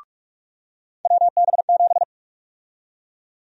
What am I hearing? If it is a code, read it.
Morse code, heard as a keyed tone, about 48 wpm, 700 Hz. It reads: WB7
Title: USA Call Sign Prefixes - 45wpm